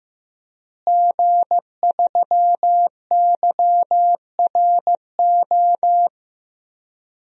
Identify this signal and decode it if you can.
Morse code, heard as a keyed tone, about 15 wpm, 700 Hz. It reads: G3YRO
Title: DXCC-100 Call Signs - 5 Characters - Repeated 3x - 15wpm